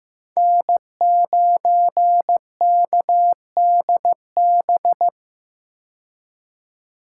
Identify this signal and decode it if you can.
Morse code, heard as a keyed tone, about 15 wpm, 700 Hz. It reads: N9KDB
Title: K1USN SST QSO Practice - 15wpm